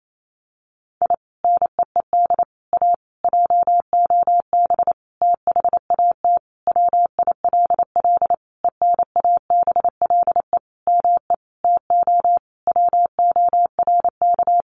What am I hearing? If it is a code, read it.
Morse code, heard as a keyed tone, about 28 wpm, 700 Hz. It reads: I NEED A JOB THAT WILL ENABLE ME TO WORK
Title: Sentences from Top 1500 Words 28wpm